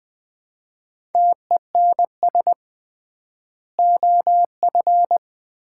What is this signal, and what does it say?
Morse code, heard as a keyed tone, about 20 wpm, 700 Hz. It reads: TENS OF
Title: 3x Word-Spacing - Sets of 4 Words - 20wpm